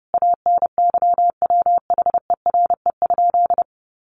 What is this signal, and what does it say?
Morse code, heard as a keyed tone, about 30 wpm, 700 Hz. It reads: ANYWHERE?
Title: Sentences from Top 1400 Words 30wpm